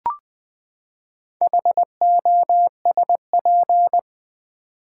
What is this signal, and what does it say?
Morse code, heard as a keyed tone, about 20 wpm, 700 Hz. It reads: HOSP
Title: Build A Word - 4 Letters At A Time - 8-Letter Words - 20wpm